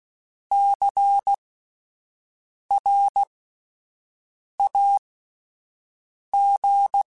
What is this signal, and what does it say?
Morse code, heard as a keyed tone, about 16 wpm, spaced as for 5 wpm, 800 Hz. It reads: CRAG
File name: Spaced Alphas / Spaced Alphas mp3s / Cs sp @10